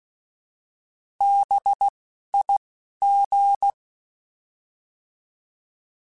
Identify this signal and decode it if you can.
Morse code, heard as a keyed tone, about 16 wpm, spaced as for 11 wpm, 800 Hz. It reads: BIG
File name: Alphabetical / Alphabetical mp3s / Bs @10